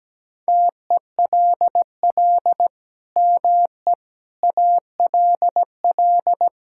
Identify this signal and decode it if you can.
Morse code, heard as a keyed tone, about 17 wpm, 700 Hz. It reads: TELL ME ALL
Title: Sentences from Top 1000 Words - Review Part 5 - 17wpm